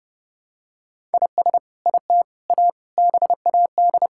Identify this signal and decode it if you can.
Morse code, heard as a keyed tone, about 32 wpm, 700 Hz. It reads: IS IT A BAD
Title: Sentences from Top 1000 Words - Review Part 5 - 30wpm